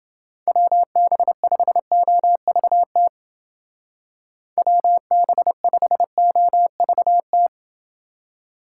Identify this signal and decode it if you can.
Morse code, heard as a keyed tone, about 29 wpm, 700 Hz. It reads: WB5OVT WB5OVT